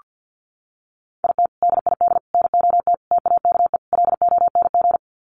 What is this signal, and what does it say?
Morse code, heard as a keyed tone, about 48 wpm, 700 Hz. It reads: IT DID NOT TAKE LONG
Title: Sets of 5 Words Encore2 50wpm